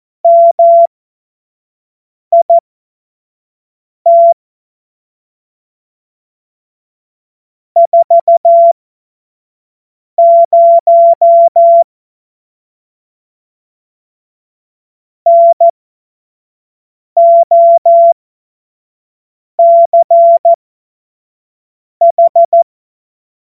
Morse code, heard as a keyed tone, about 14 wpm, spaced as for 5 wpm, 680 Hz. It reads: MIT 40 NOCH
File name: Morsekurs / Witze / Witz 19 März 2024 25 F70